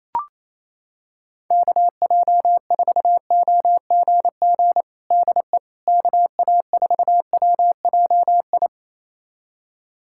Morse code, heard as a keyed tone, about 28 wpm, 700 Hz. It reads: KJ4OGG DE KA4WJI
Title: USA Callsign DE Callsign - 6 Characters - 28wpm